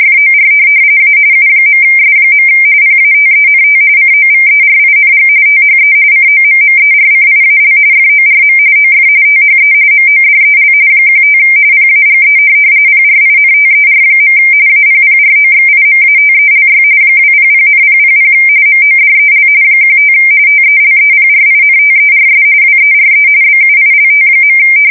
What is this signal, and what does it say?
RTTY, about 45.45 baud. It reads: FOR A U.S. PATENT COVERING THE COMMERCIAL FORM OF THE MODEL 15 PAGE PRINTER. APPROXIMATELY 200,000 MODEL 15 TELEPRINTERS WERE BUILT.
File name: fsk170 / teletype-corp